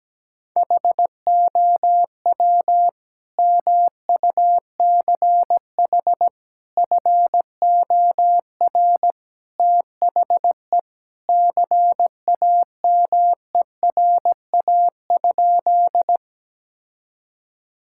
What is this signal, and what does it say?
Morse code, heard as a keyed tone, about 17 wpm, 700 Hz. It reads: HOW MUCH FOR THE CAMERA?
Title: Sentences from Top 1000 Words - Review Part 5 - 17wpm